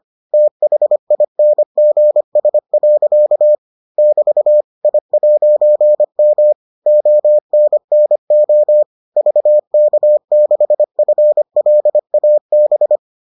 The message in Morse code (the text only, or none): THINGS. <BT> I'M ONNO VK6FLAB